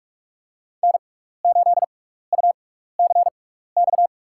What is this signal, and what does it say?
Morse code, heard as a keyed tone, about 46 wpm, spaced as for 16 wpm, 700 Hz. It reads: N8UCX
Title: USA Call Signs - 5 Characters - 6x Character Spacing - 45wpm